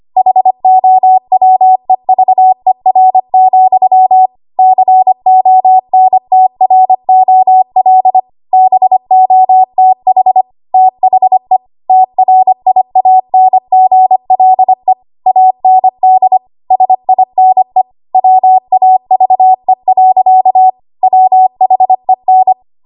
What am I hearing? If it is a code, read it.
Morse code, heard as a keyed tone, about 25 wpm, 750 Hz. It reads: HOWEVER, CONTROL BOTH THE TRIANGLE AND SINE WAVE. WHEN
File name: Archive / 25 WPM / 25WPM